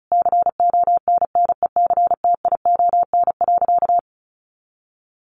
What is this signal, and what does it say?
Morse code, heard as a keyed tone, about 34 wpm, 700 Hz. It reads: CONNECTION.
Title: Sentences from Top 1300 Words 35wpm